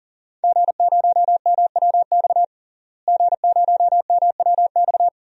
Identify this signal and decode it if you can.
Morse code, heard as a keyed tone, about 40 wpm, 700 Hz. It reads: G0MWX G0MWX
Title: DXCC-100 Call Signs - 5 Characters - Repeated 2x - 40wpm